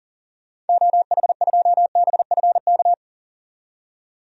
Morse code, heard as a keyed tone, about 40 wpm, 700 Hz. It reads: OH2BFK
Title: DXCC-100 Call Signs - 6 Characters - 40wpm